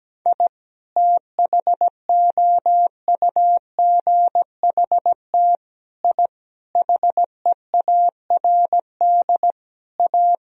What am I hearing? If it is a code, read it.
Morse code, heard as a keyed tone, about 17 wpm, 700 Hz. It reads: I THOUGHT I HEARD A
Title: Sentences from Top 1000 Words - Review Part 1 - 17wpm